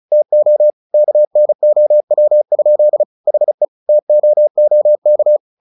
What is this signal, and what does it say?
Morse code, heard as a keyed tone, about 34 wpm, 600 Hz. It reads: TO KNOW? HE TOOK
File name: the old man and the sea / the old man and the sea 1 35wpm